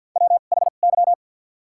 Morse code, heard as a keyed tone, about 53 wpm, spaced as for 37 wpm, 700 Hz. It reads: WHY
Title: 2x Character Spacing - Top 300 Words - 50wpm